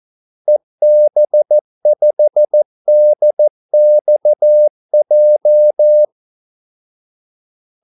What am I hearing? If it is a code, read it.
Morse code, heard as a keyed tone, about 14 wpm, 600 Hz. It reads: EB5DXJ
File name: Callsigns-14wpm